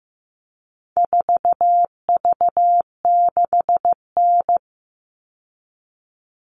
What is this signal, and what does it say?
Morse code, heard as a keyed tone, about 15 wpm, 700 Hz. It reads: 4V6N
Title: All Cut Numbers - 4-Digit 15wpm